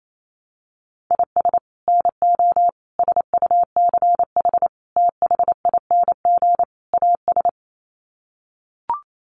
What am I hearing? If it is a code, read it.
Morse code, heard as a keyed tone, about 28 wpm, 700 Hz. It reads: IS NO SUCH THING AS
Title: Sets of 5 Words Encore 28wpm